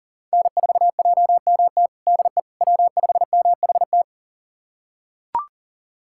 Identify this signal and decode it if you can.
Morse code, heard as a keyed tone, about 39 wpm, 700 Hz. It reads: N4JMT DE W5MHT
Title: USA Callsign DE Callsign - 5 Characters - 40wpm